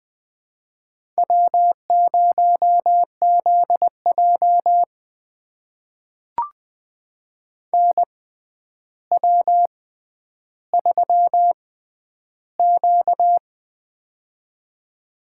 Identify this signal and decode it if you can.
Morse code, heard as a keyed tone, about 20 wpm, spaced as for 10 wpm, 700 Hz. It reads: W0ZJ NW3Q
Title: USA Call Signs - 4 Characters - 6x Character Spacing - 20wpm